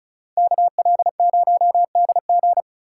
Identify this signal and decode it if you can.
Morse code, heard as a keyed tone, about 43 wpm, 700 Hz. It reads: KL0DG
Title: Speed Racing - USA 2x2 Call Signs 35wpm